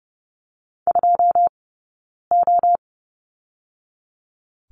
Morse code, heard as a keyed tone, about 30 wpm, spaced as for 9 wpm, 700 Hz. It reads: 2O
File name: Lesson 20d - Number 3 - Two Characters 30wpm